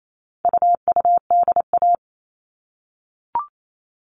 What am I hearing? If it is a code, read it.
Morse code, heard as a keyed tone, about 28 wpm, 700 Hz. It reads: UUDA
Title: All Cut Numbers - 4-Digit 28wpm